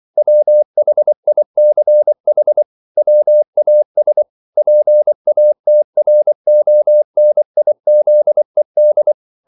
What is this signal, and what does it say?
Morse code, heard as a keyed tone, about 24 wpm, 600 Hz. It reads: WHICH WAS PATRONIZED